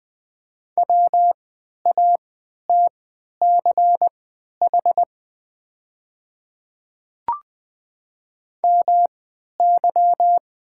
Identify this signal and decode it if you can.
Morse code, heard as a keyed tone, about 20 wpm, spaced as for 11 wpm, 700 Hz. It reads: WATCH MY
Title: 3x Character Spacing - Top 300 Words - 20wpm